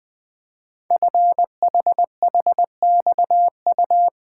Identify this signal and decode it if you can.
Morse code, heard as a keyed tone, about 20 wpm, 700 Hz. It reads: FHHXU
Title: Code Groups - 5 Letters - 20wpm